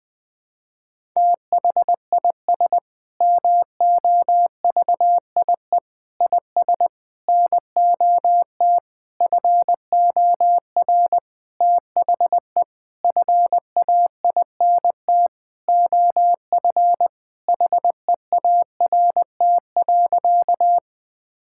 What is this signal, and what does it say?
Morse code, heard as a keyed tone, about 20 wpm, 700 Hz. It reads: THIS MOVIE IS NOT FOR THE FAINT OF HEART.
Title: Sentences with 6-Word Phrases - Spoken Beginning - 20wpm